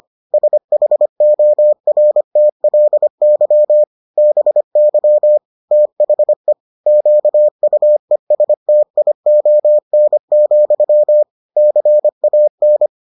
Morse code, heard as a keyed tone, about 25 wpm, 600 Hz. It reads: SHORTLY BY THE QUESTION, CAN